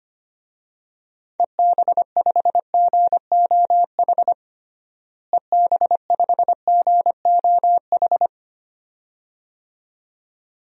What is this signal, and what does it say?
Morse code, heard as a keyed tone, about 25 wpm, 700 Hz. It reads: EB5GOH EB5GOH
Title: DXCC-100 Call Signs - 6 Characters - Repeated 2x - 25wpm